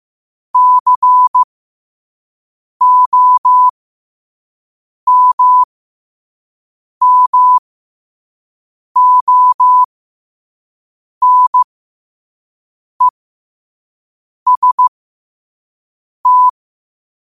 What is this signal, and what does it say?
Morse code, heard as a keyed tone, about 15 wpm, spaced as for 5 wpm, 1000 Hz. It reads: COMMONEST
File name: qotdfiles / 5wpm